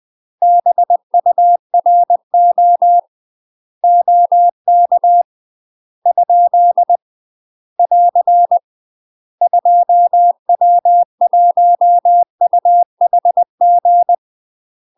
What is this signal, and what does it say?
Morse code, heard as a keyed tone, about 20 wpm, 700 Hz. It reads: BURO OK ? <AR> 2W1UHG